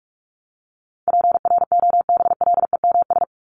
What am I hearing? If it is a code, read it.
Morse code, heard as a keyed tone, about 44 wpm, 700 Hz. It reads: PROBLEMS
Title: ES:S Suffix - Top 100 Words with Base Words 45wpm